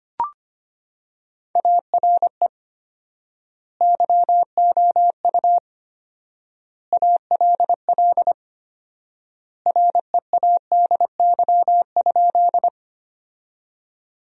Morse code, heard as a keyed tone, about 25 wpm, 700 Hz. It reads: ARE YOU ALL READY?
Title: 4x Word-Spacing - Sentences from Top 400 Words - 25wpm